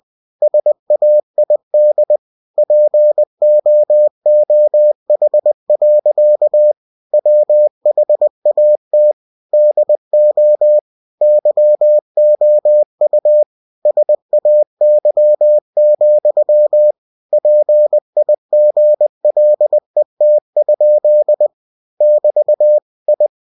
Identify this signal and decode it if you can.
Morse code, heard as a keyed tone, about 20 wpm, 600 Hz. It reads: SAID POOH. WHAT DO YOU SAY, PIGLET? <BT> I
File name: cw-books / winnie-the-pooh / winnie-20wpm-page-0110